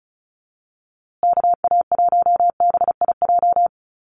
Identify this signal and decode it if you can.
Morse code, heard as a keyed tone, about 34 wpm, 700 Hz. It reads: KA1BIJ
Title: USA Call Signs Repeated 2x 35wpm